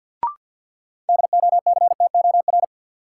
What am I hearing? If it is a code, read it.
Morse code, heard as a keyed tone, about 49 wpm, 700 Hz. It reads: DOCTOR
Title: Sentences from Top 1500 Words - New Word Drill Alt - 50wpm